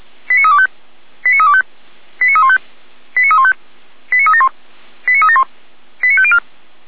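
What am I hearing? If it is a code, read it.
Five-tone selective call (ZVEI), heard as quick runs of five tones, tones of about 70 ms. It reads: 89325 89325 89315 89315 89371 89371 89504
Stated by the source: Five-tone ZVEI selective call received by SDR.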